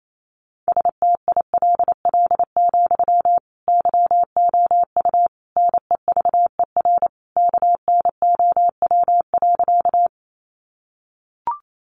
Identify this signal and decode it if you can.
Morse code, heard as a keyed tone, about 28 wpm, 700 Hz. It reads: STILL, YOU NEVER KNOW.
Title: Sentences from Top 300 Words 28wpm